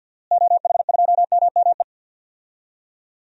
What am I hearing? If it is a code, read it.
Morse code, heard as a keyed tone, about 48 wpm, 700 Hz. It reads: OH2MME
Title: DXCC-100 Call Signs - 6 Characters - Repeated 3x - 50wpm